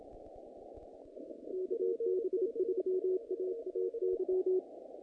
Morse code, heard as a keyed tone, about 26 wpm, 370 Hz. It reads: KB3AAY
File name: contests / naqp-cw-k5tr-op-k5pi- -2100